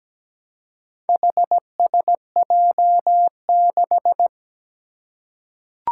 Morse code, heard as a keyed tone, about 17 wpm, 700 Hz. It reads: HSJ6
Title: Four Challenging Letter-Number Combinations 17wpm